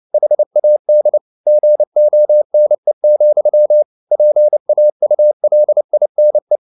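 Morse code, heard as a keyed tone, about 29 wpm, 600 Hz. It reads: HAD GONE, PAULINE
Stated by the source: Morse code at 29 wpm, sending Ken Follett's novel Never in English, from CW med Gnister.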